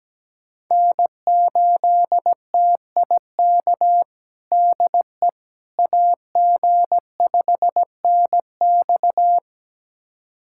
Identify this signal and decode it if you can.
Morse code, heard as a keyed tone, about 17 wpm, 700 Hz. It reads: N8TIK DE AG5NX
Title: USA Callsign DE Callsign - 5 Characters - 17wpm